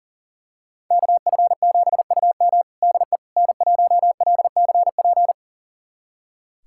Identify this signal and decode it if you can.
Morse code, heard as a keyed tone, about 40 wpm, 700 Hz. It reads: KF7UM DE N1LCP
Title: USA Callsign DE Callsign - 5 Characters - 40wpm